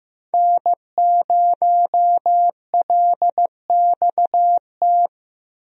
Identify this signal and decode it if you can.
Morse code, heard as a keyed tone, about 15 wpm, 700 Hz. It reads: N0LXT
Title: USA Call Signs - 5 Characters - Repeated 3x - 15wpm